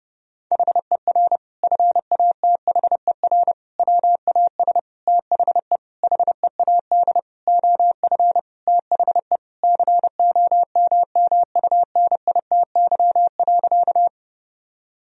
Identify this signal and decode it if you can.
Morse code, heard as a keyed tone, about 30 wpm, 700 Hz. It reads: HER FATHER WAS THE HEAD OF THE COMMUNITY.
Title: Sentences from Top 1000 Words - Review Part 4 - 30wpm